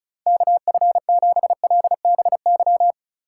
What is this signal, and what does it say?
Morse code, heard as a keyed tone, about 36 wpm, 700 Hz. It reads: KF7LBY
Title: USA Call Signs - 6 Characters - 4x Character Spacing - 35wpm